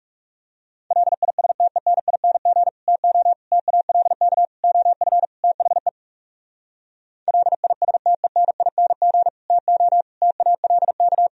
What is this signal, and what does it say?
Morse code, heard as a keyed tone, about 40 wpm, 700 Hz. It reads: LISTENING TO TALK OF THE LISTENING TO TALK